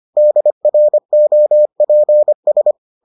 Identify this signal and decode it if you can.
Morse code, heard as a keyed tone, about 25 wpm, 600 Hz. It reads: DROPS